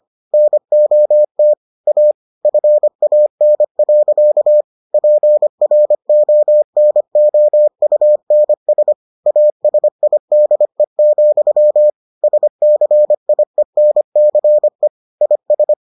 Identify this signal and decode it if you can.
Morse code, heard as a keyed tone, about 25 wpm, 600 Hz. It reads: NOT A FAN. PRONOUNS ASIDE, SCIENCE IS